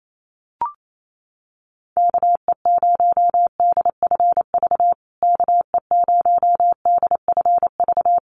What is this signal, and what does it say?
Morse code, heard as a keyed tone, about 28 wpm, 700 Hz. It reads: KE0DFV KE0DFV
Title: USA Call Signs Repeated 3x 28wpm